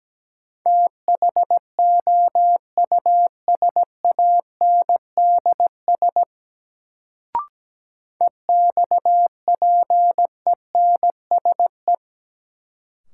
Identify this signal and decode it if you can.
Morse code, heard as a keyed tone, about 17 wpm, 700 Hz. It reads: THOUSANDS EXPENSE
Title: Sets of 4 Words - New Word Drill - 17wpm